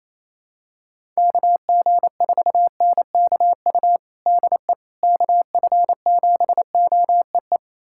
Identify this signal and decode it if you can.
Morse code, heard as a keyed tone, about 28 wpm, 700 Hz. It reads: KG4NKU DE KF7OEE
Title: USA Callsign DE Callsign - 6 Characters - 28wpm